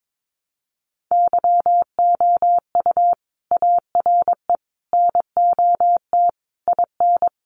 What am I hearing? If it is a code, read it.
Morse code, heard as a keyed tone, about 22 wpm, 700 Hz. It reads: YOU ARE NOT IN
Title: Sentences from Top 1300 Words 22wpm